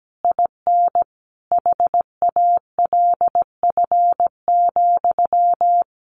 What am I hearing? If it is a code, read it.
Morse code, heard as a keyed tone, about 17 wpm, 700 Hz. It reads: IN HALF,
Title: A Sound Of Thunder 17wpm